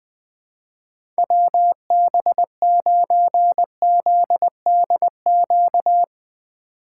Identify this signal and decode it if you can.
Morse code, heard as a keyed tone, about 22 wpm, 700 Hz. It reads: WB9ZDQ
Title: Speed Racing - USA Call Signs 20wpm